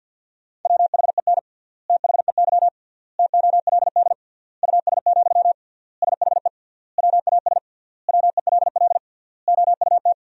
Morse code, heard as a keyed tone, about 50 wpm, 700 Hz. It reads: WHEN THEY TOLD US, SHE WAS WELL OUT
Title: 3x Word-Spacing - Sentences from Top 400 Words - 50wpm